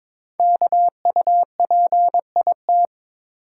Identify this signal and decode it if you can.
Morse code, heard as a keyed tone, about 22 wpm, 700 Hz. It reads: KUPIT
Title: Code Groups - 5 Letter:Numbers - 22wpm